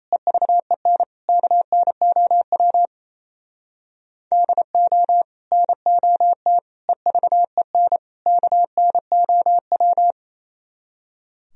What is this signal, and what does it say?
Morse code, heard as a keyed tone, about 36 wpm, 700 Hz. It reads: EVEN KNOW DO NOT EVEN KNOW